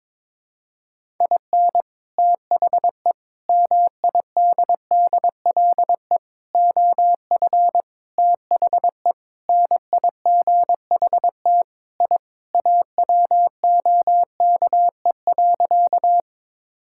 Morse code, heard as a keyed tone, about 22 wpm, 700 Hz. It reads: IN THE MIDDLE OF THE NIGHT I AWOKE.
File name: practice / 22wpm / Sentences with 4-Word Phrases - 22wpm